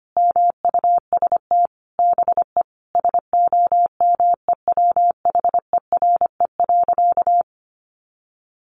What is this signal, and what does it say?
Morse code, heard as a keyed tone, about 25 wpm, 700 Hz. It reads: MUST BE SOMEWHERE.
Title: Sentences from Top 1400 Words 25wpm